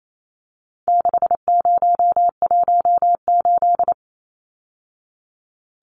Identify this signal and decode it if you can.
Morse code, heard as a keyed tone, about 28 wpm, 700 Hz. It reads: 6018
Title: Numbers - 4-Digit 28wpm